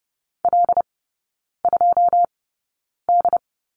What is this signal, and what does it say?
Morse code, heard as a keyed tone, about 30 wpm, spaced as for 9 wpm, 700 Hz. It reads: L2D
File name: 081 Lesson 14e - Number 2 - Three Characters 30wpm